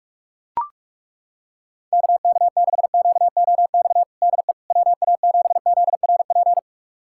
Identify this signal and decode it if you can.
Morse code, heard as a keyed tone, about 42 wpm, 700 Hz. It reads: KK6QOX DE WA7ZRP
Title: USA Callsign DE Callsign - 6 Characters - 45wpm